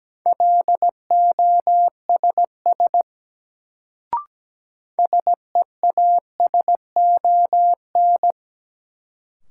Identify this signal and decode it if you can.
Morse code, heard as a keyed tone, about 17 wpm, 700 Hz. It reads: LOSS SEASON
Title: Sentences from Top 1400 Words - New Word Drill Alt - 17wpm